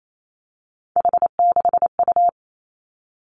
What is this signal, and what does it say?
Morse code, heard as a keyed tone, about 28 wpm, 700 Hz. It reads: H6U
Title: Three Challenging Letter-Number Combinations 28wpm